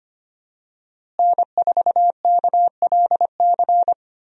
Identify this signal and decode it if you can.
Morse code, heard as a keyed tone, about 25 wpm, 700 Hz. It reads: N4KLC
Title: USA Call Signs - 5 Characters - Repeated 2x - 25wpm